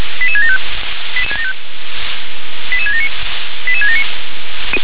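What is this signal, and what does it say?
Five-tone selective call (ZVEI), heard as quick runs of five tones, tones of about 70 ms. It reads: 88675 88679 88679
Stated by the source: ZVEI five-tone selective call received by SDR.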